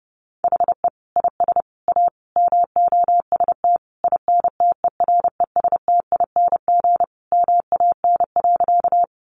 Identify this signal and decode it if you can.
Morse code, heard as a keyed tone, about 30 wpm, 700 Hz. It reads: HE IS A MOST INTERESTING MAN.
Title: Sentences from Top 1000 Words 30wpm